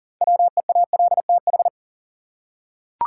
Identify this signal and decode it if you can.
Morse code, heard as a keyed tone, about 40 wpm, 700 Hz. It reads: WEALTH
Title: Six Letter Words - 40wpm